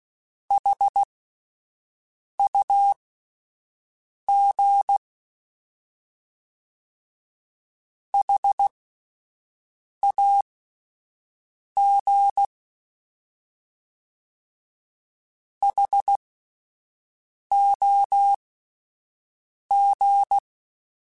Morse code, heard as a keyed tone, about 16 wpm, spaced as for 5 wpm, 800 Hz. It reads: HUG HAG HOG